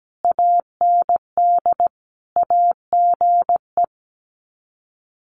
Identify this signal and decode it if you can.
Morse code, heard as a keyed tone, about 17 wpm, 700 Hz. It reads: AND AGE
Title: Sets of 5 Words Encore 17wpm